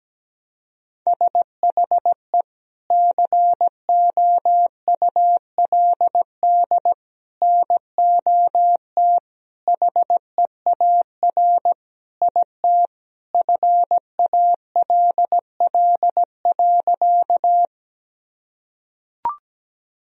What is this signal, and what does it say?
Morse code, heard as a keyed tone, about 17 wpm, 700 Hz. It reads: SHE COULD NOT HEAR IT FALL.
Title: Sentences from Top 1000 Words - Review Part 2 - 17wpm